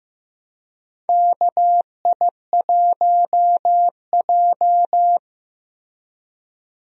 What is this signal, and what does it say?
Morse code, heard as a keyed tone, about 15 wpm, 700 Hz. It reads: KI1J
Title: USA Call Signs - 4 Characters - Repeated 2x - 15wpm